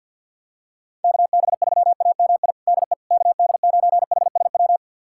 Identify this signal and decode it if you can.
Morse code, heard as a keyed tone, about 51 wpm, 700 Hz. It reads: KB3AMI DE KD9HSW
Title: USA Callsign DE Callsign - 6 Characters - 50wpm